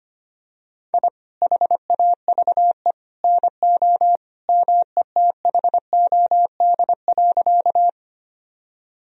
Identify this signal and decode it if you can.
Morse code, heard as a keyed tone, about 25 wpm, 700 Hz. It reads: I HAVE NO METHOD.
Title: Sentences from Top 1000 Words - Review Part 3 - 25wpm